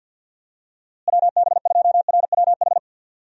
Morse code, heard as a keyed tone, about 47 wpm, 700 Hz. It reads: WB2RWH
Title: USA Call Signs - 6 Characters - Repeated 3x - 50wpm